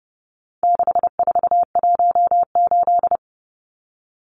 Morse code, heard as a keyed tone, about 30 wpm, 700 Hz. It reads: 6418
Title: Numbers - 4-Digit 30wpm